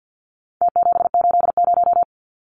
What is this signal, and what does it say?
Morse code, heard as a keyed tone, about 50 wpm, 700 Hz. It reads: T780
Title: Numbers - Contest Style - 4-Digit 50wpm